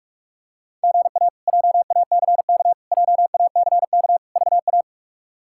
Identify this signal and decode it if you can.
Morse code, heard as a keyed tone, about 44 wpm, 700 Hz. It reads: GA JACK JACK VA
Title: K1USN SST QSO Practice - 45wpm